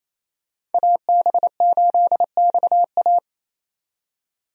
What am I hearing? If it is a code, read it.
Morse code, heard as a keyed tone, about 28 wpm, 700 Hz. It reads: AB8XA
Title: USA Call Signs - 5 Characters - 28wpm